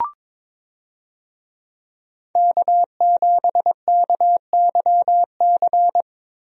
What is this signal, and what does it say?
Morse code, heard as a keyed tone, about 22 wpm, 700 Hz. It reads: K7KYC
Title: USA Call Signs - 5 Characters - Repeated 2x - 22wpm